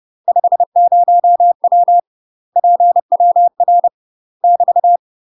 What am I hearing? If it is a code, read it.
Morse code, heard as a keyed tone, about 30 wpm, 700 Hz. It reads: 50W PWR <BT>
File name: UK MASTER QSOs TRAINING / 30WPM / UK MASTER QSOs 0011